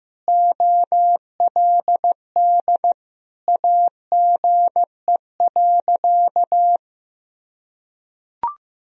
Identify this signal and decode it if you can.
Morse code, heard as a keyed tone, about 15 wpm, 700 Hz. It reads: OLD AGE.
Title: Sentences from Top 1000 Words - Review Part 3 - 15wpm